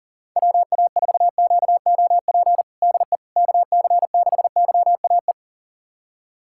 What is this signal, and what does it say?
Morse code, heard as a keyed tone, about 37 wpm, 700 Hz. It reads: WA4QOP DE KC6YAE
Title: USA Callsign DE Callsign 40wpm